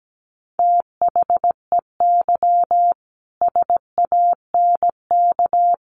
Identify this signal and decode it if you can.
Morse code, heard as a keyed tone, about 17 wpm, 700 Hz. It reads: THEY SANK
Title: A Sound Of Thunder 17wpm